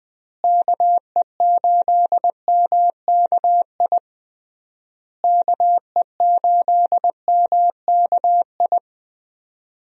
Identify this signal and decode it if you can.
Morse code, heard as a keyed tone, about 20 wpm, 700 Hz. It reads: KE8MKI KE8MKI
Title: USA Call Signs - 6 Characters - Repeated 3x - 20wpm